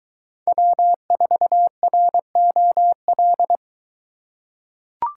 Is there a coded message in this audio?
Yes: Morse code from 0.5 to 3.6 s, heard as a keyed tone, about 22 wpm, 700 Hz. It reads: W4ROL